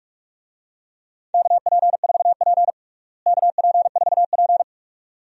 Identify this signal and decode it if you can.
Morse code, heard as a keyed tone, about 42 wpm, 700 Hz. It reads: KP4P KP4P
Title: DXCC-100 Call Signs - 4 Characters - Repeated 2x - 45wpm